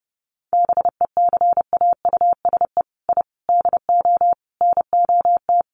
Morse code, heard as a keyed tone, about 30 wpm, 700 Hz. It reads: BECAUSE I DO NOT
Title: Sets of 5 Words Encore 30wpm